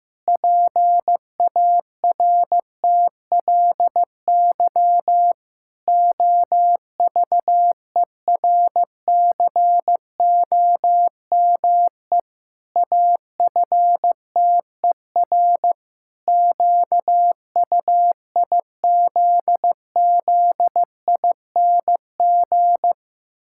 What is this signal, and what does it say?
Morse code, heard as a keyed tone, about 15 wpm, 700 Hz. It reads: PARTLY OVERCOME AFTER QUIZZING
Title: Pangram Challenge v2 - 15wpm